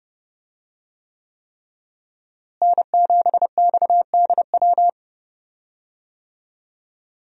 Morse code, heard as a keyed tone, about 31 wpm, 700 Hz. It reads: N7XDW
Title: USA Call Signs - 5 Characters - Repeated 3x - 30wpm